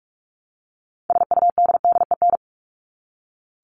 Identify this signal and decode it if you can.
Morse code, heard as a keyed tone, about 44 wpm, 700 Hz. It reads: SUDDEN
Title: LY Suffix - Top 100 Words with Base Words 45wpm